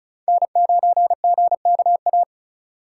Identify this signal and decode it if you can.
Morse code, heard as a keyed tone, about 36 wpm, 700 Hz. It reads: N9GKA
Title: USA Call Signs - 5 Characters - 2x Character Spacing - 35wpm